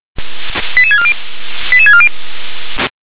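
Five-tone selective call (ZVEI), heard as quick runs of five tones, tones of about 70 ms. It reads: 88640 88640
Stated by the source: Five-tone ZVEI selective call, received via SDR.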